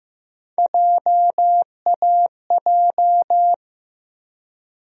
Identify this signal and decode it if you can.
Morse code, heard as a keyed tone, about 16 wpm, 700 Hz. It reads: JAJ
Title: USA Call Sign Postfixes - 3 Letters 15wpm